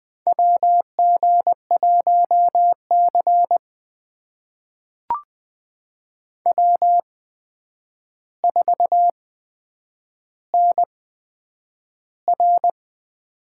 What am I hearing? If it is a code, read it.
Morse code, heard as a keyed tone, about 20 wpm, spaced as for 9 wpm, 700 Hz. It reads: WG1C W4NR
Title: USA Call Signs - 4 Characters - 8x Character Spacing - 20wpm